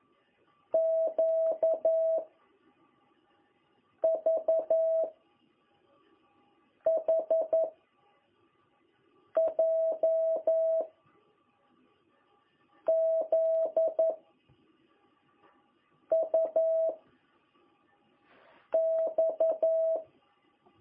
Morse code, heard as a keyed tone, about 11 wpm, 660 Hz. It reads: Q V H J Z U X
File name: jh8cbh / 2021cw